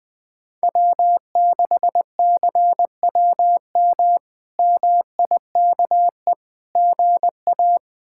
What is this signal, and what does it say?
Morse code, heard as a keyed tone, about 20 wpm, 700 Hz. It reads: W6CWM MIKE GA
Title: K1USN SST QSO Practice - 20wpm